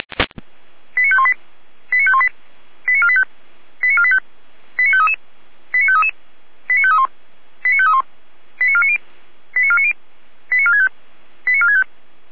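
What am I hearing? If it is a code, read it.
Five-tone selective call (ZVEI), heard as quick runs of five tones, tones of about 80 ms. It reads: 89518 89518 89485 89485 89533 89533 89531 89531 89490 89490 89465 89465
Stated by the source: ZVEI five-tone call received by SDR.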